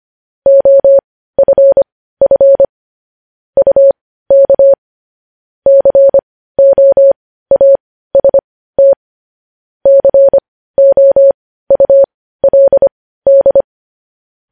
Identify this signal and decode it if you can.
Morse code, heard as a keyed tone, about 25 wpm, spaced as for 15 wpm, 550 Hz. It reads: OFF UK COAST COULD
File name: podcast / News headlines-1x-25x15